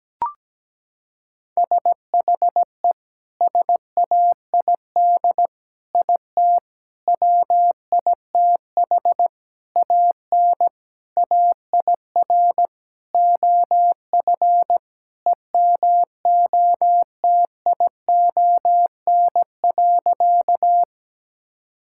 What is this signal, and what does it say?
Morse code, heard as a keyed tone, about 17 wpm, 700 Hz. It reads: SHE SAID IT WITH AN AIR OF EMOTION.
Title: Sentences from Top 1600 Words 17wpm